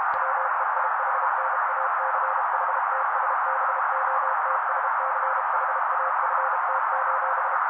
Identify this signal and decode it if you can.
Morse code, heard as a keyed tone, about 32 wpm, 530 Hz. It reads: WEIHNACHTSBOTSCHAFT:
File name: daily / 31wpm noise